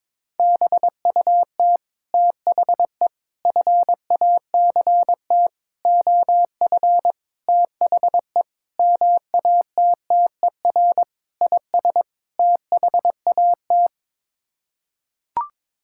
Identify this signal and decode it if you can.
Morse code, heard as a keyed tone, about 22 wpm, 700 Hz. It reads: BUT THE FACT OF THE MATTER IS THAT
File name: practice / 22wpm / Sets of 8 Words - 22wpm